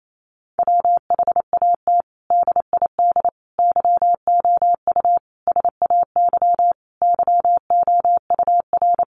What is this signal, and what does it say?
Morse code, heard as a keyed tone, about 28 wpm, 700 Hz. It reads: WHAT DID YOU SAY YOUR
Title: Sentences from Top 200 Words 28wpm